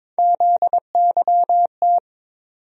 Morse code, heard as a keyed tone, about 22 wpm, 700 Hz. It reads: ZYT
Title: USA Call Sign Postfixes - 3 Letters 22wpm